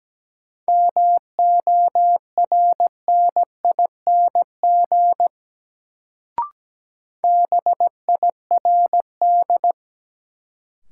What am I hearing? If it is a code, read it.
Morse code, heard as a keyed tone, about 17 wpm, 700 Hz. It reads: MORNING BIRD
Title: Sentences from Top 400 Words - New Word Drill - 17wpm